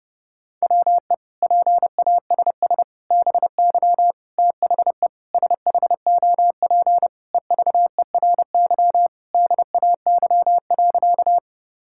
Morse code, heard as a keyed tone, about 30 wpm, 700 Hz. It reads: WE PASS BY THE SHOP EVERY DAY.
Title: Sentences from Top 1600 Words 30wpm